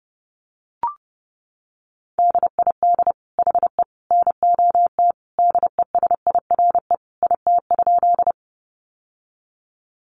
Morse code, heard as a keyed tone, about 30 wpm, 700 Hz. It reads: DID HE NOT DESIRE IT?
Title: Sentences from Top 1100 Words 30wpm